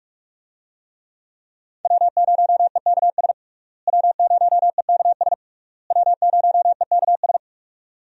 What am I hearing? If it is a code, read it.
Morse code, heard as a keyed tone, about 42 wpm, 700 Hz. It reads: W0EKS W0EKS W0EKS
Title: USA Call Signs - 5 Characters - Repeated 3x - 45wpm